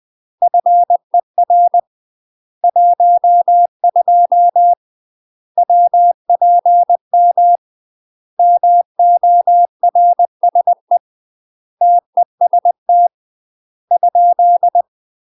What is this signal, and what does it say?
Morse code, heard as a keyed tone, about 20 wpm, 700 Hz. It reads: FER 12 WPM MORSE TEST ?
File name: UK MASTER QSOs TRAINING / 20WPM / UK MASTER QSOs 0071